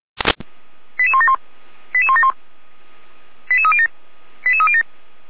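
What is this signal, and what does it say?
Five-tone selective call (ZVEI), heard as quick runs of five tones, tones of about 70 ms. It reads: 88171 88171 88307 88307
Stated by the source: ZVEI five-tone call 89099, received by SDR, from the feuerwehr-orlamuende.org fire brigade audio server.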